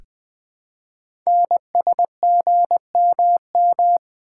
Morse code, heard as a keyed tone, about 20 wpm, 700 Hz. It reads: NSGMM
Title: Code Groups - 5 Letters - 20wpm